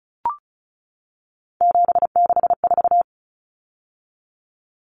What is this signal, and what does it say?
Morse code, heard as a keyed tone, about 34 wpm, 700 Hz. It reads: 764
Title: Numbers - 3-Digit 35wpm